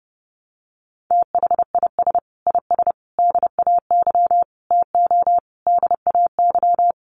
Morse code, heard as a keyed tone, about 30 wpm, 700 Hz. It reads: THIS IS DAY TO DAY
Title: Sets of 5 Words Encore 30wpm